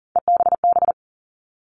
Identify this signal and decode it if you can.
Morse code, heard as a keyed tone, about 40 wpm, 700 Hz. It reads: EBB
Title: All Cut Numbers - 3-Digit 40wpm